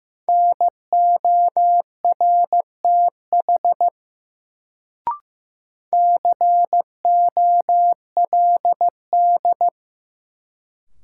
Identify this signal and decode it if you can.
Morse code, heard as a keyed tone, about 15 wpm, 700 Hz. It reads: NORTH COLD
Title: Sentences from Top 400 Words - New Word Drill - 15wpm